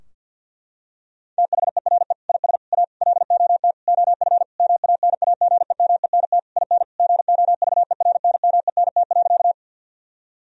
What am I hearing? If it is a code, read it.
Morse code, heard as a keyed tone, about 49 wpm, 700 Hz. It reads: THERE IS A LOT OF MANAGEMENT IN GOVERNMENT.
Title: Sentences from Top 1000 Words - Review Part 4 - 50wpm